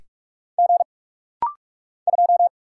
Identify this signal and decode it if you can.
Morse code, heard as a keyed tone, about 44 wpm, 700 Hz. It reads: G 2
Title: Single Letter-Number - Rapid-Fire 45wpm